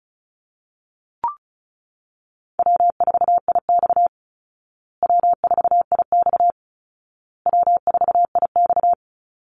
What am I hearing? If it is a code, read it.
Morse code, heard as a keyed tone, about 34 wpm, 700 Hz. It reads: W4IX W4IX W4IX
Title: Easy Call Signs Repeated 3x - 35wpm